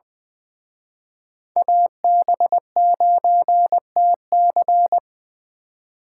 Morse code, heard as a keyed tone, about 25 wpm, 700 Hz. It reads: AB9TC
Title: Speed Racing - USA 2x2 Call Signs 20wpm